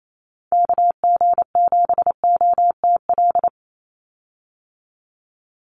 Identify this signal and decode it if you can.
Morse code, heard as a keyed tone, about 28 wpm, 700 Hz. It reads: KG7OTL
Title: USA Call Signs 28wpm